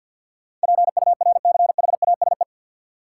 Morse code, heard as a keyed tone, about 53 wpm, 700 Hz. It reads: PURCHASE
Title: Sentences from Top 800 Words - New Word Drill - 50wpm